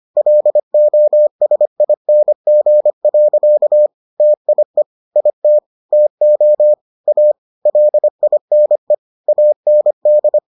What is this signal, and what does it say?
Morse code, heard as a keyed tone, about 25 wpm, 600 Hz. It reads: LOSING. TIE IT TO A LINE AND